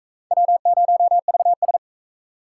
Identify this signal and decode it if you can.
Morse code, heard as a keyed tone, about 37 wpm, 700 Hz. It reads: W0VS